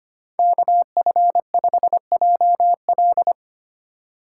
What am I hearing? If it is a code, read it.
Morse code, heard as a keyed tone, about 25 wpm, 700 Hz. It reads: KF5JL
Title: USA Call Signs - 5 Characters - 2x Character Spacing - 25wpm